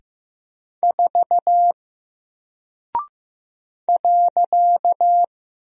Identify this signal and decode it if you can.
Morse code, heard as a keyed tone, about 15 wpm, 700 Hz. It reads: 4 .
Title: Single Character (Letter:Number:Punctuation:Prosign) 15wpm